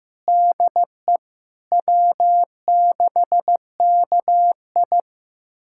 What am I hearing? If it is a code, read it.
Morse code, heard as a keyed tone, about 15 wpm, 700 Hz. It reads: DE W6KI
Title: USA Callsign DE Callsign - 4 Characters - 15wpm